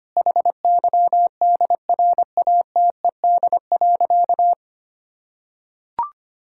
Morse code, heard as a keyed tone, about 25 wpm, 700 Hz. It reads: HYDRATED.
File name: practice / 25wpm / Difficult Sentences with Hints - 25wpm